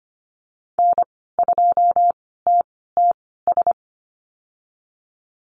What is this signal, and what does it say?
Morse code, heard as a keyed tone, about 25 wpm, spaced as for 16 wpm, 700 Hz. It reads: N2TTS
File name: Lesson 15g - Number 5 - Call Signs 25wpm